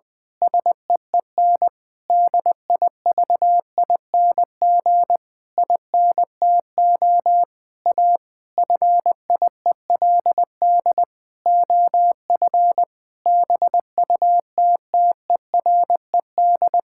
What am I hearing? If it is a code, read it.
Morse code, heard as a keyed tone, about 20 wpm, 700 Hz. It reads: SEEN DIVING INTO A FIELD OF BUTTERED
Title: Pangram Challenge v2 - 20wpm